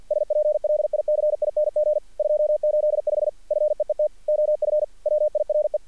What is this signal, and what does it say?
Morse code, heard as a keyed tone, about 48 wpm, 610 Hz. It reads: SPLICING 285 FEET OF WIRE
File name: archived / CRSnet